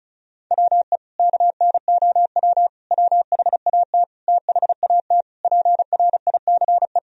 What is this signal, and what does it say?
Morse code, heard as a keyed tone, about 36 wpm, 700 Hz. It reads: WE KNOW WHAT THAT PRICE
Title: Sentences from Top 1000 Words - Review Part 4 - 35wpm